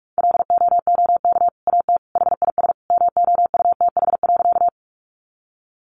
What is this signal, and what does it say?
Morse code, heard as a keyed tone, about 44 wpm, 700 Hz. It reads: LOOK AT HIS MOUTH.
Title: Sentences from Top 1500 Words 45wpm